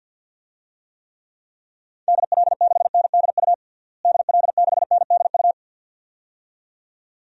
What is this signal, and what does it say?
Morse code, heard as a keyed tone, about 47 wpm, 700 Hz. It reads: DL6NDU DL6NDU
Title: DXCC-100 Call Signs - 6 Characters - Repeated 2x - 50wpm